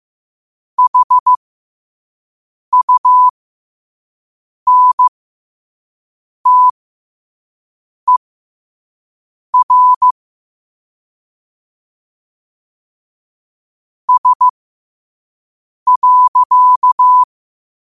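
Morse code, heard as a keyed tone, about 15 wpm, spaced as for 5 wpm, 1000 Hz. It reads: HUNTER S.
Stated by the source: 5 WPM Morse quote of the day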